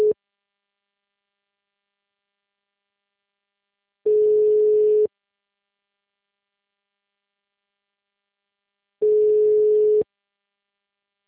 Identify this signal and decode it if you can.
Ringback tone (Europe, 425 Hz).